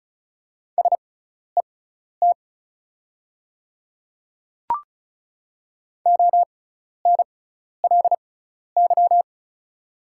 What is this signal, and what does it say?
Morse code, heard as a keyed tone, about 36 wpm, spaced as for 12 wpm, 690 Hz. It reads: SET ONLY
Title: 6x Character Spacing - Top 200 Words - 35wpm